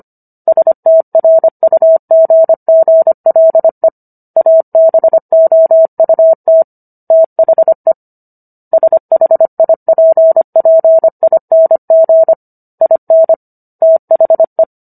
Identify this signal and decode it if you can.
Morse code, heard as a keyed tone, about 25 wpm, 650 Hz. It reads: STRUGGLE ABOUT THE SHIPPING IN THE